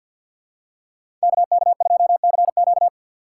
Morse code, heard as a keyed tone, about 46 wpm, 700 Hz. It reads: KK2CQ
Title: USA Call Signs - 5 Characters - 6x Character Spacing - 50wpm